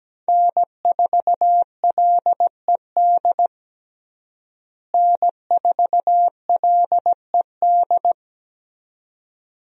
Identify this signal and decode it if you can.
Morse code, heard as a keyed tone, about 17 wpm, 700 Hz. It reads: N4LED N4LED
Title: USA Call Signs - 5 Characters - Repeated 2x - 17wpm